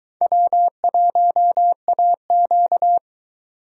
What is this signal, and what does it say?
Morse code, heard as a keyed tone, about 22 wpm, 700 Hz. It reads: W1AQ